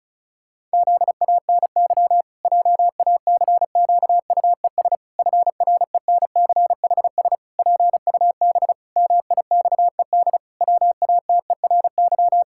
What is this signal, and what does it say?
Morse code, heard as a keyed tone, about 36 wpm, 700 Hz. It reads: ZANY JACQUES FRENCHS PUB MIXED WATERY
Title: Pangram Challenge v2 - 35wpm